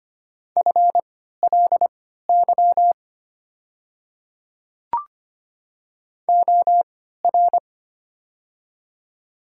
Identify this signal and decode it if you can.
Morse code, heard as a keyed tone, about 25 wpm, spaced as for 14 wpm, 700 Hz. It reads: FLY OR
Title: 3x Character Spacing - Top 500 Words - 25wpm